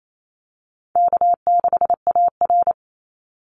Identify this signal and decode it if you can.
Morse code, heard as a keyed tone, about 28 wpm, 700 Hz. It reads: K6AR
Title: Easy Call Signs - 28wpm